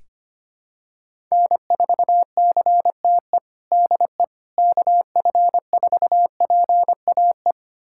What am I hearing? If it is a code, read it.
Morse code, heard as a keyed tone, about 25 wpm, 700 Hz. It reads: N4CTE DE KF4PAE
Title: USA Callsign DE Callsign 25wpm